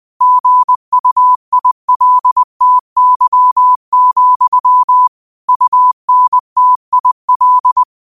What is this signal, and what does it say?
Morse code, heard as a keyed tone, about 20 wpm, 1000 Hz. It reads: GUILTY, UNTIL
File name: 20wpm